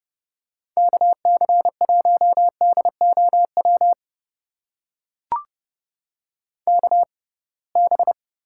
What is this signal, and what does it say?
Morse code, heard as a keyed tone, about 31 wpm, spaced as for 23 wpm, 700 Hz. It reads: KC1DOW KB
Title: USA Call Signs - 6 Characters - 6x Character Spacing - 30wpm